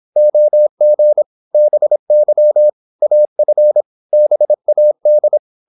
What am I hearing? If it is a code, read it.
Morse code, heard as a keyed tone, about 26 wpm, 600 Hz. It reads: OG BY AF BAD
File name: Korte ord DK 26wpm